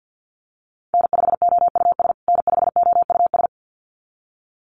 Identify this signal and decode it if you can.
Morse code, heard as a keyed tone, about 49 wpm, 700 Hz. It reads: N5OUS N5OUS
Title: USA Call Signs Repeated 2x 50wpm